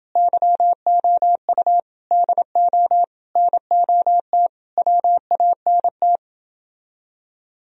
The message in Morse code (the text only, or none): YOU DO NOT WANT